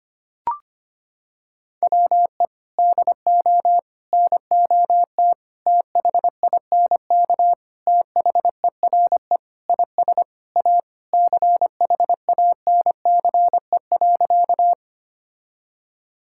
Morse code, heard as a keyed tone, about 25 wpm, 700 Hz. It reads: WE DO NOT THINK THERE IS A CHANCE.
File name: practice / 25wpm / Sentences with 4-Word Phrases - 25wpm